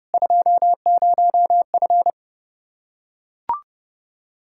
Morse code, heard as a keyed tone, about 30 wpm, 700 Hz. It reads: 20F